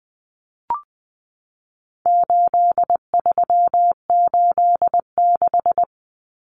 Morse code, heard as a keyed tone, about 20 wpm, 700 Hz. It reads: 8386
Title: Numbers - 4-Digit 20wpm